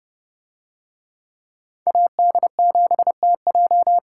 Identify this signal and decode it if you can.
Morse code, heard as a keyed tone, about 30 wpm, 700 Hz. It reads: AD7TJ